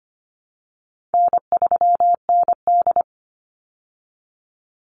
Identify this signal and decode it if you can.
Morse code, heard as a keyed tone, about 25 wpm, 700 Hz. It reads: N3ND
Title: Easy Call Signs Repeated 2x - 25wpm